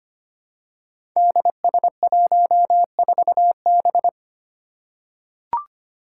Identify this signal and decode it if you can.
Morse code, heard as a keyed tone, about 25 wpm, 700 Hz. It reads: DS14B
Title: Five Challenging Letter-Number Combinations 25wpm